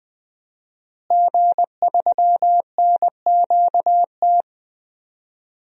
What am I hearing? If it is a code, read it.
Morse code, heard as a keyed tone, about 20 wpm, 700 Hz. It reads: G3NQT
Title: DXCC-100 Call Signs - 5 Characters - Repeated 2x - 20wpm